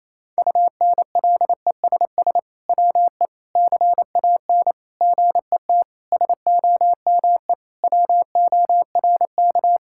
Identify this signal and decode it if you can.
Morse code, heard as a keyed tone, about 27 wpm, 700 Hz. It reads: UNLESS WE CAN GET SOME WORK
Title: Sentences from Top 1000 Words - Review Part 6 - 28wpm